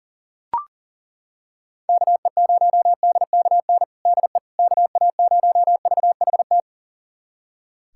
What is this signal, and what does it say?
Morse code, heard as a keyed tone, about 40 wpm, 700 Hz. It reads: KE0DKN DE KA0VHT
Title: USA Callsign DE Callsign - 6 Characters - 40wpm